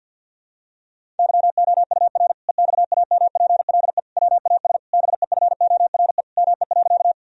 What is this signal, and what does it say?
Morse code, heard as a keyed tone, about 46 wpm, 700 Hz. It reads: YOUR EXAMPLE WAS BEFORE ME.
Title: Sentences from Top 1000 Words - Review Part 1 - 50wpm